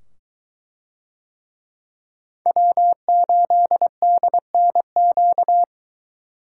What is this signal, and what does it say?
Morse code, heard as a keyed tone, about 21 wpm, 700 Hz. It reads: W8DNQ